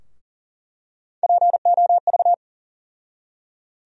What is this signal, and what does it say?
Morse code, heard as a keyed tone, about 40 wpm, 700 Hz. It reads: POV
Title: USA Call Sign Postfixes - 3 Letters 40wpm